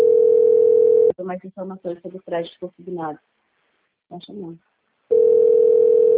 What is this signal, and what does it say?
Ringback tone (North America), 440 + 480 Hz, with speech.
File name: out- - - - - .28909